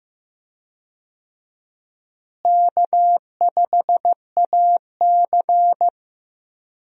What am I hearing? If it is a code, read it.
Morse code, heard as a keyed tone, about 16 wpm, 700 Hz. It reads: K5AC
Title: USA Call Signs - 4 Characters - Repeated 2x - 15wpm